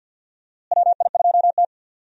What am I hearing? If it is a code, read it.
Morse code, heard as a keyed tone, about 50 wpm, 700 Hz. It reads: WI2T
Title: USA Call Signs - 4 Characters - Repeated 2x - 50wpm